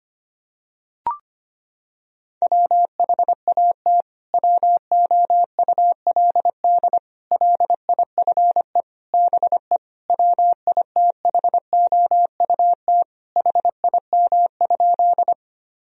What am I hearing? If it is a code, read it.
Morse code, heard as a keyed tone, about 25 wpm, 700 Hz. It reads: WHAT WOULD LIFE BE WITHOUT HIM?
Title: Sentences from Top 1000 Words - Review Part 3 - 25wpm